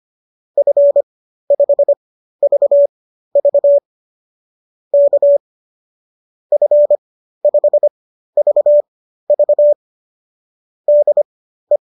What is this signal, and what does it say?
Morse code, heard as a keyed tone, about 25 wpm, spaced as for 13 wpm, 590 Hz. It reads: F5VV K F5VV DE